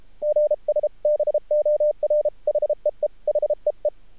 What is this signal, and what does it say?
Morse code, heard as a keyed tone, about 33 wpm, 600 Hz. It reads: GS BOR HEE HEE
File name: archived / CRSnet